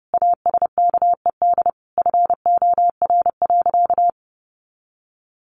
Morse code, heard as a keyed tone, about 30 wpm, 700 Hz. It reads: ASKED FOR.
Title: A Sound Of Thunder 30wpm